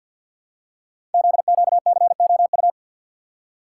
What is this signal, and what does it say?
Morse code, heard as a keyed tone, about 54 wpm, 700 Hz. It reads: ZQCOU
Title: Code Groups - 5 Letters - 50wpm